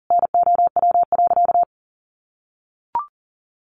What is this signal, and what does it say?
Morse code, heard as a keyed tone, about 37 wpm, 700 Hz. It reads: NOW.
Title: Sentences from Top 1300 Words 40wpm